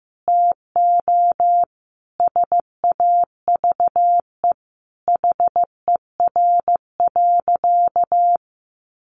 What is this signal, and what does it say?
Morse code, heard as a keyed tone, about 14 wpm, 700 Hz. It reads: TO SAVE HER.
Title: Sentences from Top 1100 Words 15wpm